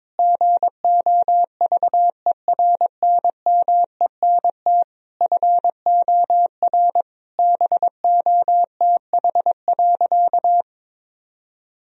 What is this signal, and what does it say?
Morse code, heard as a keyed tone, about 22 wpm, 700 Hz. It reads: GOVERNMENT FOR BOTH.
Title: Sentences from Top 1000 Words - Review Part 3 - 22wpm